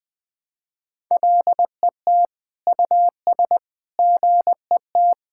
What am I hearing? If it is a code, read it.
Morse code, heard as a keyed tone, about 21 wpm, 700 Hz. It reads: LET US GET
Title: Sentences with 7-Word Phrases - Spoken Beginning - 20wpm